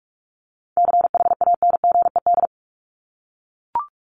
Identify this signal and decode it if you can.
Morse code, heard as a keyed tone, about 44 wpm, 700 Hz. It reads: CHANGED
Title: ED Suffix - Top 100 Words with Base Words 45wpm